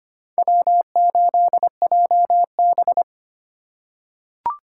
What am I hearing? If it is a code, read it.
Morse code, heard as a keyed tone, about 25 wpm, 700 Hz. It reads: W8JB
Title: USA Call Signs - 4 Characters - 3x Character Spacing - 25wpm